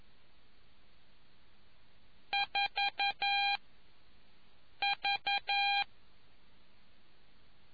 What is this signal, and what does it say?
Morse code, heard as a keyed tone, about 11 wpm, 800 Hz. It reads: 4 V